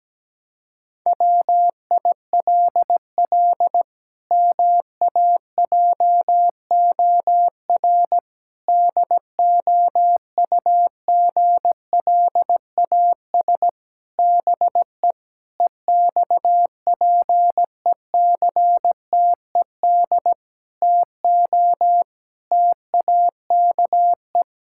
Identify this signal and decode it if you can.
Morse code, heard as a keyed tone, about 17 wpm, 700 Hz. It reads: WILL MAJOR DOUGLAS BE EXPECTED TO TAKE
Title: Pangram Challenge v2 - 17wpm